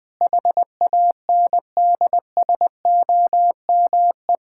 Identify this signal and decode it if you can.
Morse code, heard as a keyed tone, about 20 wpm, 700 Hz. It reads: HANDSOME
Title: Pangram Challenge v2 - 20wpm